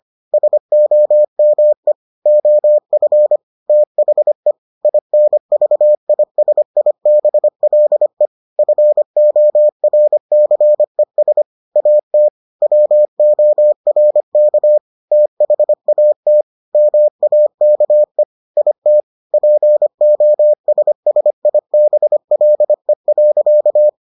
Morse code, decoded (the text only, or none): SOME OF THE INVISIBLE FORCES AT WORK THAT MAKE IT POSSIBLE.